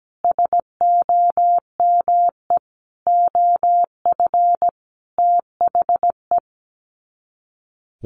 Morse code, heard as a keyed tone, about 17 wpm, 700 Hz. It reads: SOME OF THE
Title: Sets of 5 Words Encore 17wpm